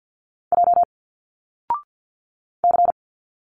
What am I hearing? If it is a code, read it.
Morse code, heard as a keyed tone, about 50 wpm, 700 Hz. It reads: J C
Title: Single Letters 50wpm